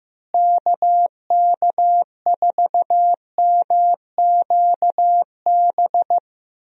Morse code, heard as a keyed tone, about 15 wpm, 700 Hz. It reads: KK4MQB
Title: USA Call Signs - 6 Characters - Repeated 2x - 15wpm